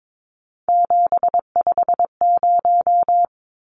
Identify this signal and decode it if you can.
Morse code, heard as a keyed tone, about 22 wpm, 700 Hz. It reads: 750
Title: Numbers - 3-Digit 22wpm